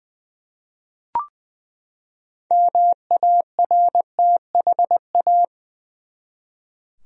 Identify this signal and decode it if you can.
Morse code, heard as a keyed tone, about 20 wpm, 700 Hz. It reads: MARTHA
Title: USA Common Names - 20wpm